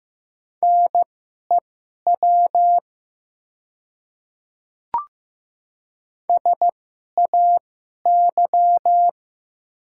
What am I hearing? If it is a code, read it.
Morse code, heard as a keyed tone, about 16 wpm, spaced as for 11 wpm, 700 Hz. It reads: NEW SAY
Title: 2x Character Spacing - Top 200 Words - 15wpm